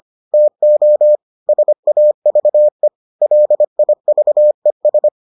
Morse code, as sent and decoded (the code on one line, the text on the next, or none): - --- / ... .- ...- . / .-.. .. ...- . ...
TO SAVE LIVES